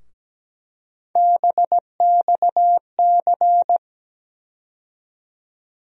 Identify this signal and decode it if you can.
Morse code, heard as a keyed tone, about 17 wpm, 700 Hz. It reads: BXC
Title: USA Call Sign Postfixes - 3 Letters 17wpm